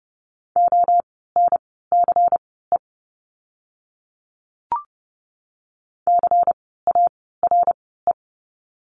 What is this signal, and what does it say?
Morse code, heard as a keyed tone, about 30 wpm, spaced as for 17 wpm, 700 Hz. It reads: ONCE CARE
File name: Lesson 17f - Letter u - Words 30wpm